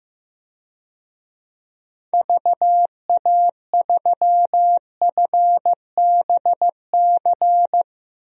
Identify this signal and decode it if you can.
Morse code, heard as a keyed tone, about 15 wpm, 700 Hz. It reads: VA3FBC
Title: DXCC-100 Call Signs - 6 Characters - Repeated 2x - 15wpm